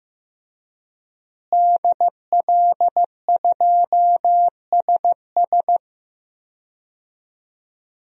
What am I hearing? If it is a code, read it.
Morse code, heard as a keyed tone, about 15 wpm, 700 Hz. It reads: DL2SS
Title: DXCC-100 Call Signs - 5 Characters - Repeated 3x - 15wpm